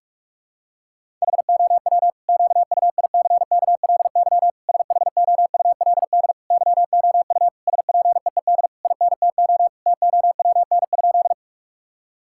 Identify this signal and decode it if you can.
Morse code, heard as a keyed tone, about 46 wpm, 700 Hz. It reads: HOW QUICKLY SHOULD YOU SPEED INTO TOWN?
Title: Sentences from Top 1000 Words - Review Part 5 - 45wpm